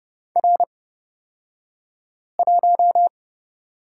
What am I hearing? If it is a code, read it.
Morse code, heard as a keyed tone, about 30 wpm, spaced as for 5 wpm, 700 Hz. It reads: R1
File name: Lesson 38d - Letter q - Two Characters 30wpm